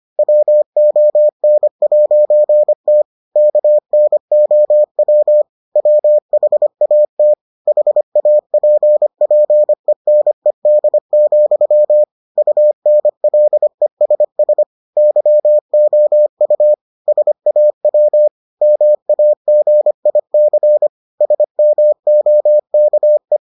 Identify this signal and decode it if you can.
Morse code, heard as a keyed tone, about 25 wpm, 600 Hz. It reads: WON'T KNOW WHAT HAPPENED, UNLESS YOU SAW MAGIC SMOKE